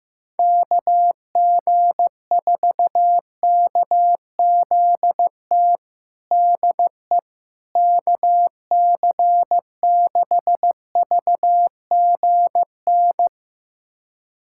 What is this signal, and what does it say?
Morse code, heard as a keyed tone, about 15 wpm, 700 Hz. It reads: KG4KZT DE KC6VGN
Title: USA Callsign DE Callsign 15wpm